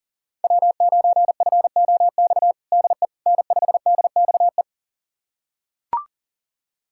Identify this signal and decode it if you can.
Morse code, heard as a keyed tone, about 37 wpm, 700 Hz. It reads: W9FOX DE N5DXE
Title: USA Callsign DE Callsign - 5 Characters - 40wpm